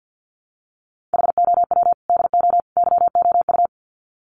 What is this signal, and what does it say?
Morse code, heard as a keyed tone, about 50 wpm, 700 Hz. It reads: HOW DO YOU
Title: Sets of 3 Words 50wpm